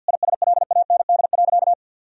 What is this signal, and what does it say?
Morse code, heard as a keyed tone, about 50 wpm, 700 Hz. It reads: ISLAND.
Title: Sentences from Top 1000 Words - Review Part 3 - 50wpm